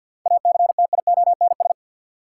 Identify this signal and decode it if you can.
Morse code, heard as a keyed tone, about 53 wpm, 700 Hz. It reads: ACTIONS
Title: Sentences from Top 1000 Words - Review Part 6 - New Word Drill - 50wpm